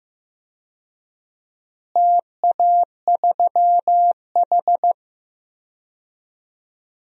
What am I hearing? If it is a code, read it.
Morse code, heard as a keyed tone, about 15 wpm, 700 Hz. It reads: TA3H
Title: DXCC-100 Call Signs - 4 Characters - Repeated 2x - 15wpm